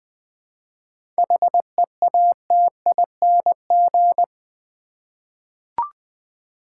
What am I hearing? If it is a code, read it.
Morse code, heard as a keyed tone, about 20 wpm, 690 Hz. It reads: HEATING
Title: 7 Letter Words - First 3 Letters Given - 20wpm